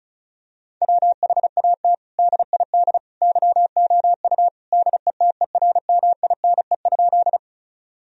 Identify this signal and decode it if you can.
Morse code, heard as a keyed tone, about 36 wpm, 700 Hz. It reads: WHAT DID YOU DETERMINE?
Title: Sentences from Top 1000 Words - Review Part 4 - 35wpm